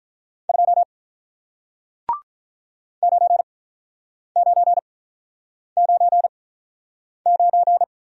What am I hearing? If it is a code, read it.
Morse code, heard as a keyed tone, about 48 wpm, spaced as for 9 wpm, 700 Hz. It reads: 2 9999